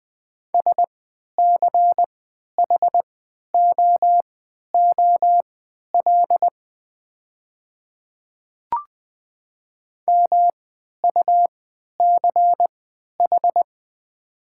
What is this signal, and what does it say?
Morse code, heard as a keyed tone, about 20 wpm, spaced as for 11 wpm, 700 Hz. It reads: SCHOOL MUCH
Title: 3x Character Spacing - Top 500 Words - 20wpm